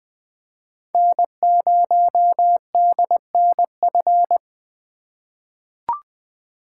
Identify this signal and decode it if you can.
Morse code, heard as a keyed tone, about 21 wpm, 700 Hz. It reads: N0DNF
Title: 2023 SOTA Chaser Call Signs - 20wpm